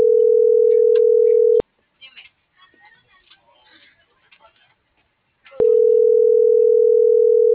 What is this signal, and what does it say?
Ringback tone (North America), 440 + 480 Hz.